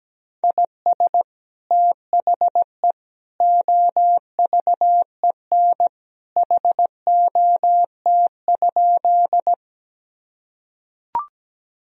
Morse code, heard as a keyed tone, about 17 wpm, 700 Hz. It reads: IS THE OVEN HOT?
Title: Sentences from Top 1000 Words - Review Part 4 - 17wpm